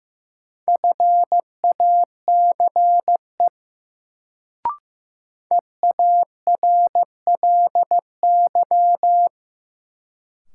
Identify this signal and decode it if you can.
Morse code, heard as a keyed tone, about 15 wpm, 700 Hz. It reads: FACE EARLY
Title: Sets of 3 Words - New Word Drill - 15wpm